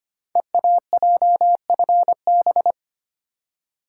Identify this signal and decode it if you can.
Morse code, heard as a keyed tone, about 25 wpm, 700 Hz. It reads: EAJFB
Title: Code Groups - 5 Letter:Numbers - 25wpm